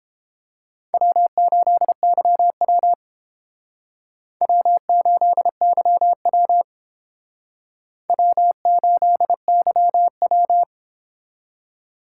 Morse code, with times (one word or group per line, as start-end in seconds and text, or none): 0.9-2.9 W8YW
4.4-6.6 W8YW
8.1-10.6 W8YW